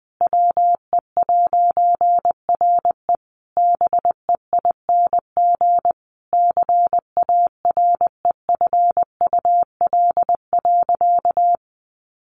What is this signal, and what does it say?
Morse code, heard as a keyed tone, about 19 wpm, 700 Hz. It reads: WE'RE BEING CAREFUL.
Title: A Sound Of Thunder 20wpm